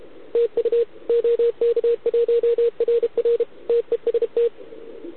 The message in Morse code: TU OK1RR TEST